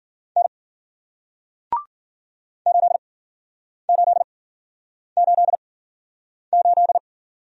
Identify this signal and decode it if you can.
Morse code, heard as a keyed tone, about 55 wpm, 700 Hz. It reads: N 8 8 8 8